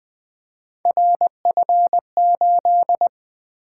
Morse code, heard as a keyed tone, about 20 wpm, 700 Hz. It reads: RF8
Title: Three Letter-Number Combinations - 20wpm